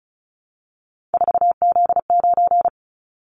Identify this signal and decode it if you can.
Morse code, heard as a keyed tone, about 36 wpm, 690 Hz. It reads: 4Z9
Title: Callsign Prefixes for Top 50 DXCC - 35wpm